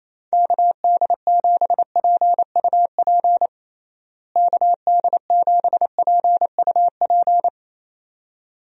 Morse code, heard as a keyed tone, about 28 wpm, 700 Hz. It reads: KD7PUP KD7PUP
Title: USA Call Signs - 6 Characters - Repeated 2x - 28wpm